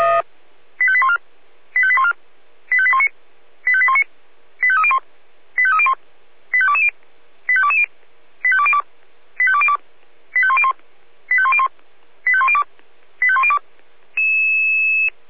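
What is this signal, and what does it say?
Five-tone selective call (ZVEI), heard as quick runs of five tones, tones of about 70 ms. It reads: 85814 85814 85819 85819 85301 85301 85220 85220 85202 85202 85101 85101 85102 85102
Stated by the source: ZVEI five-tone call received on SDR.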